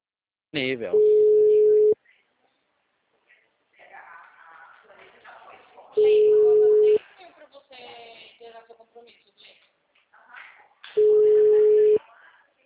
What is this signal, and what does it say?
Ringback tone (Europe, 425 Hz), with speech.